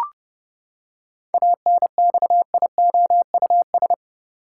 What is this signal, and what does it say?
Morse code, heard as a keyed tone, about 30 wpm, 700 Hz. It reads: ANXIOUS
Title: Seven Letter Words - 30wpm